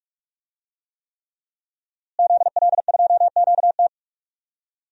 Morse code, heard as a keyed tone, about 41 wpm, 700 Hz. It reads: ZL2QT
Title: DXCC-100 Call Signs - 5 Characters - Repeated 3x - 45wpm